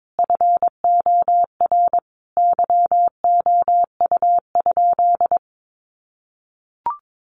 Morse code, heard as a keyed tone, about 22 wpm, 700 Hz. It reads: FOR YOU?
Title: Sentences from Top 1400 Words 22wpm